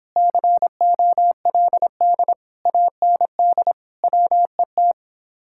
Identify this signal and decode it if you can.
Morse code, heard as a keyed tone, about 31 wpm, 700 Hz. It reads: COLD AND WET